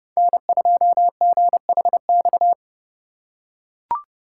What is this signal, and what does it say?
Morse code, heard as a keyed tone, about 29 wpm, 700 Hz. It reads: N2GHX